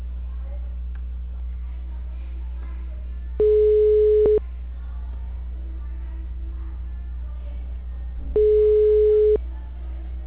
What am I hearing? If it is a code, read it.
Ringback tone (Europe, 425 Hz), with speech.
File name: out- -2020- - - .8214